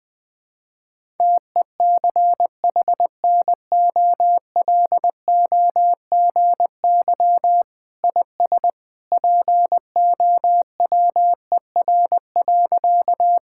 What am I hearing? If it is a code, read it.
Morse code, heard as a keyed tone, about 20 wpm, 700 Hz. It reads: TECHNOLOGY IS POWER.
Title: Sentences from Top 1000 Words - Review Part 5 - 20wpm